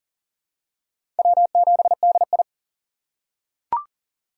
Morse code, heard as a keyed tone, about 40 wpm, 700 Hz. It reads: W7DI
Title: USA Call Signs - 4 Characters - 3x Character Spacing - 40wpm